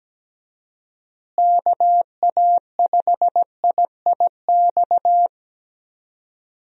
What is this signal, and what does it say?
Morse code, heard as a keyed tone, about 17 wpm, 700 Hz. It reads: KA5IIX
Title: USA Call Signs - 6 Characters - Repeated 3x - 17wpm